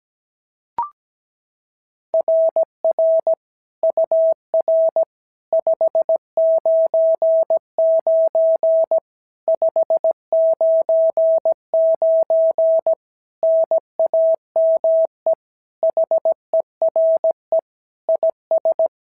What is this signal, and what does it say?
Morse code, heard as a keyed tone, about 16 wpm, 650 Hz. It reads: RR UR 599 599 NAME HERE IS
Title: SOTA QSO Practice 17wpm